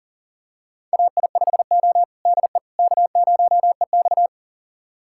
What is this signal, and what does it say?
Morse code, heard as a keyed tone, about 40 wpm, 700 Hz. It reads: AI5O DE K0EX
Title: USA Callsign DE Callsign - 4 Characters - 40wpm